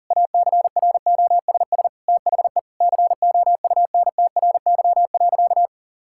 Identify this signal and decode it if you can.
Morse code, heard as a keyed tone, about 40 wpm, 700 Hz. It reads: ACROSS THE COUNTRY.
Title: Sentences from Top 1000 Words - Review Part 5 - 40wpm